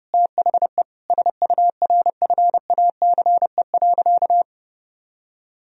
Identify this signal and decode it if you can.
Morse code, heard as a keyed tone, about 30 wpm, 700 Hz. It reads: THE SURFACE.
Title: Sentences from Top 1000 Words - Review Part 3 - 30wpm